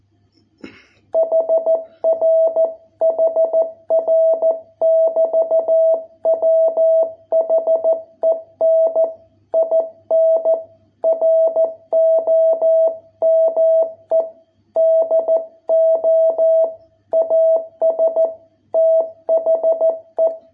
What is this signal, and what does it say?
Morse code, heard as a keyed tone, about 14 wpm, 660 Hz. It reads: HR HR <BT> WHEN IN ROME DO AS THE